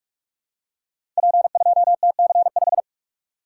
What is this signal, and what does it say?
Morse code, heard as a keyed tone, about 44 wpm, 690 Hz. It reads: P2TC5
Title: Code Groups - 5 Letter:Numbers - 45wpm